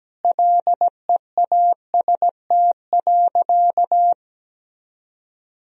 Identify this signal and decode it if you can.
Morse code, heard as a keyed tone, about 17 wpm, 700 Hz. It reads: LEAST.
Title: Sentences from Top 1000 Words - Review Part 2 - 17wpm